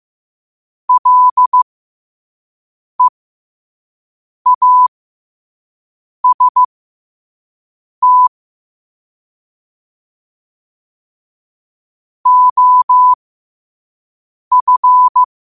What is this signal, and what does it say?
Morse code, heard as a keyed tone, about 15 wpm, spaced as for 5 wpm, 1000 Hz. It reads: LEAST OF
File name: qotdfiles / 5wpm